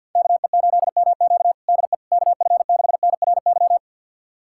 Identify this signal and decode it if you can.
Morse code, heard as a keyed tone, about 51 wpm, 700 Hz. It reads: KE8MQ DE KF6NRY
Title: USA Callsign DE Callsign 50wpm